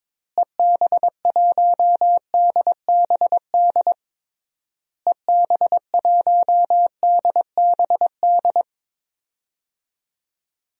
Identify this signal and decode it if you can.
Morse code, heard as a keyed tone, about 22 wpm, 700 Hz. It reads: EB1DBD EB1DBD
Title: DXCC-100 Call Signs - 6 Characters - Repeated 3x - 22wpm